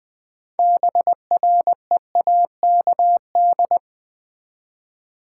Morse code, heard as a keyed tone, about 20 wpm, 700 Hz. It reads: BREAKD
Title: Build A Word - 3 Letters At A Time - 9-Letter Words - 20wpm